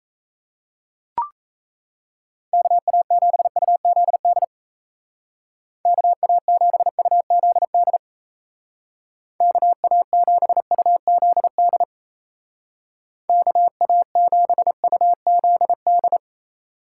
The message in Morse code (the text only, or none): KA7UZD KA7UZD KA7UZD KA7UZD